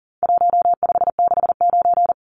Morse code, heard as a keyed tone, about 38 wpm, 700 Hz. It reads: 1569
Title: Numbers - Contest Style - 4-Digit 40wpm